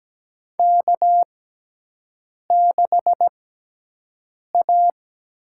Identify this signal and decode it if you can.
Morse code, heard as a keyed tone, about 17 wpm, spaced as for 6 wpm, 700 Hz. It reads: K6A
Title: USA Call Signs - 6x Character Spacing - 17wpm